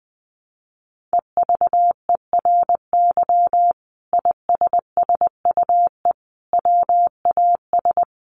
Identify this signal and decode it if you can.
Morse code, heard as a keyed tone, about 20 wpm, 700 Hz. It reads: EVERY ISSUE WAS
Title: Sentences from Top 1400 Words 20wpm